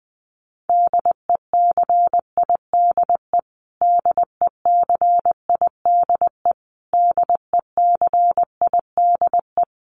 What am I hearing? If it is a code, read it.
Morse code, heard as a keyed tone, about 20 wpm, 700 Hz. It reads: DECIDE DECIDE DECIDE
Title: Top 500 Words Repeated 20wpm